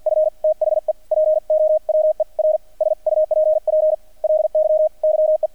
Morse code, heard as a keyed tone, about 50 wpm, spaced as for 41 wpm, 620 Hz. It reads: F THE POWER SUPPLY YE
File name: archived / CRSnet